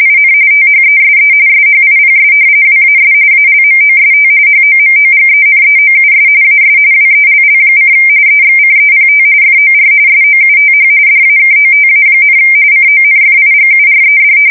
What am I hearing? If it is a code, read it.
RTTY, about 45.45 baud. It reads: IN THESE SHIPS WAS ALMOST HOMOGENEOUS AS WELL: EXCEPT FOR SOUTH DAKOTA, CONFIGURED
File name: fsk170 / battleships